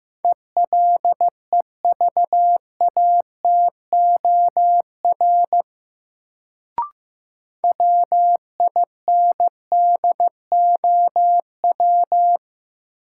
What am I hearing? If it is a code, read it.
Morse code, heard as a keyed tone, about 15 wpm, 700 Hz. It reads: ELEVATOR WINDOW
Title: Sentences from Top 1600 Words - New Word Drill - 15wpm